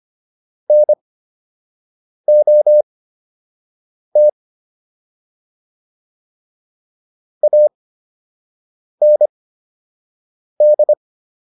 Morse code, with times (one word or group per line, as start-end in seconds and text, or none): 0.7-4.3 NOT
7.4-10.9 AND